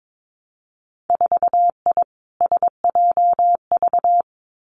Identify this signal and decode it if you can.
Morse code, heard as a keyed tone, about 22 wpm, 700 Hz. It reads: 4I SJV
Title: Winter Field Day Exchanges Encore 22wpm